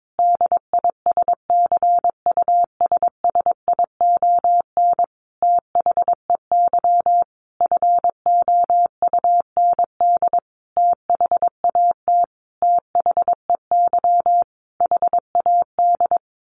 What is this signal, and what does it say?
Morse code, heard as a keyed tone, about 22 wpm, 700 Hz. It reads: DISCUSSION THEY FOUND THAT THEY HAD
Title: Sentences from Top 1200 Words 22wpm